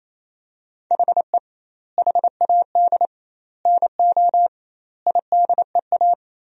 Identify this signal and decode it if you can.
Morse code, heard as a keyed tone, about 28 wpm, 700 Hz. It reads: HE HAD NO IDEA
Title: 2x Word-Spacing - Sentences from Top 300 Words - 28wpm